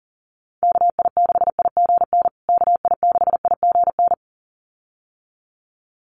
Morse code, heard as a keyed tone, about 40 wpm, 700 Hz. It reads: KI6IGN KI6IGN
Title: USA Call Signs Repeated 2x 40wpm